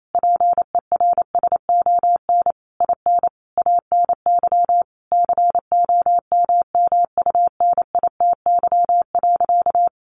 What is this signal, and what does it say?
Morse code, heard as a keyed tone, about 28 wpm, 700 Hz. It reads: PERSON IN ANY COMMUNITY.
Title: Sentences from Top 1200 Words 28wpm